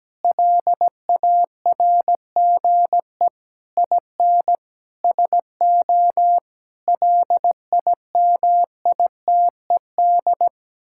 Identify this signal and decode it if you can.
Morse code, heard as a keyed tone, about 17 wpm, 700 Hz. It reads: LARGE IN SO LIMITED
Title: Sentences from Top 1000 Words - Review Part 6 - 17wpm